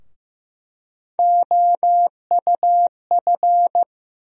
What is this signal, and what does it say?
Morse code, heard as a keyed tone, about 16 wpm, 700 Hz. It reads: OUF
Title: USA Call Sign Postfixes - 3 Letters 15wpm